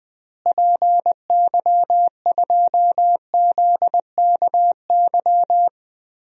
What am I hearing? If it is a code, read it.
Morse code, heard as a keyed tone, about 20 wpm, 700 Hz. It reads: PY2ZKY
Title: DXCC-100 Call Signs - 6 Characters - Repeated 3x - 20wpm